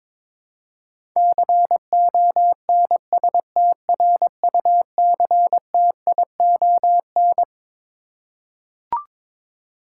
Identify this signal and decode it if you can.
Morse code, heard as a keyed tone, about 22 wpm, 700 Hz. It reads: CONSTRUCTION
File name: Twelve Letter Words - 22wpm